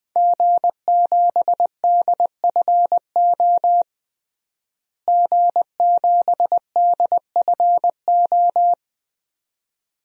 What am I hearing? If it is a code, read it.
Morse code, heard as a keyed tone, about 21 wpm, 700 Hz. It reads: G7DFO G7DFO
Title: DXCC-100 Call Signs - 5 Characters - Repeated 2x - 20wpm